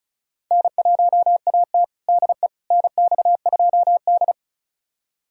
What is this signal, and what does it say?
Morse code, heard as a keyed tone, about 36 wpm, 700 Hz. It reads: N1AT DE NX2D
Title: USA Callsign DE Callsign - 4 Characters - 35wpm